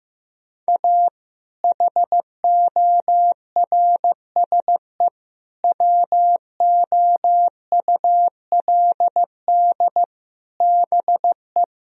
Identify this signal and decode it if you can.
Morse code, heard as a keyed tone, about 15 wpm, 700 Hz. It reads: A HORSE WOULD BE
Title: Sentences from Top 1000 Words - Review Part 1 - 15wpm